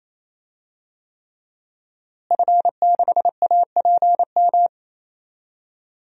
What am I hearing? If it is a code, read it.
Morse code, heard as a keyed tone, about 27 wpm, 700 Hz. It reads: F6APM
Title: DXCC-100 Call Signs - 5 Characters - Repeated 3x - 28wpm